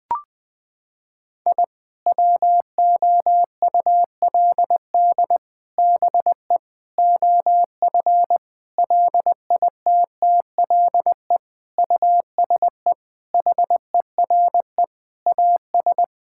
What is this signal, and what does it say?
Morse code, heard as a keyed tone, about 20 wpm, 700 Hz. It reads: I WOULD BE OF LITTLE USE HERE AS
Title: Sentences from Top 1000 Words - Review Part 5 - 20wpm